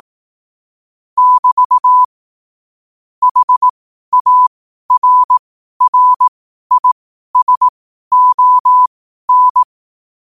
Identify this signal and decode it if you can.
Morse code, heard as a keyed tone, about 18 wpm, spaced as for 12 wpm, 1000 Hz. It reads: <BT> HARRISON
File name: qotdfiles / 12wpm